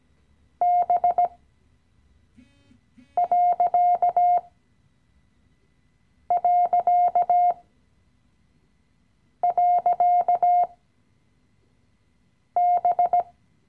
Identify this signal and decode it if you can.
Morse code, heard as a keyed tone, about 17 wpm, spaced as for 4 wpm, 710 Hz. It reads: B...B
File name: CW-Online-Kurs / Anfänger / Lektion 015